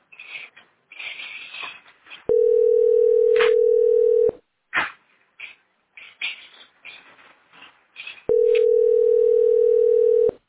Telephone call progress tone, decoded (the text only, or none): ringback tone